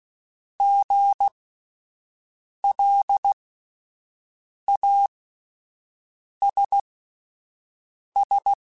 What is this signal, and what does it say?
Morse code, heard as a keyed tone, about 16 wpm, spaced as for 5 wpm, 800 Hz. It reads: GLASS